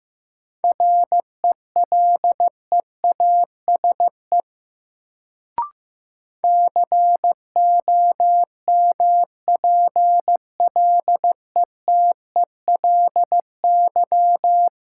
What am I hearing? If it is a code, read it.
Morse code, heard as a keyed tone, about 15 wpm, 700 Hz. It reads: RELEASE COMPLETELY
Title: Sentences from Top 1100 Words - New Word Drill - 15wpm